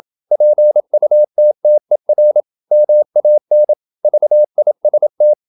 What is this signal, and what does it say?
Morse code, heard as a keyed tone, about 27 wpm, 600 Hz. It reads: PUTTER MAN VIST
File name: Ferie QRQ 27wpm